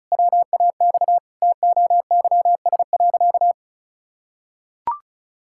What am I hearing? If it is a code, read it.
Morse code, heard as a keyed tone, about 34 wpm, 700 Hz. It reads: WAX TOYS.
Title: Pangram Challenge v2 - 35wpm